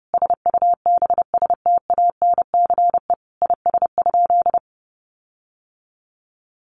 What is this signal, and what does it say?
Morse code, heard as a keyed tone, about 30 wpm, 700 Hz. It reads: SUBSTANCE IS?
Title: Sentences from Top 1500 Words 30wpm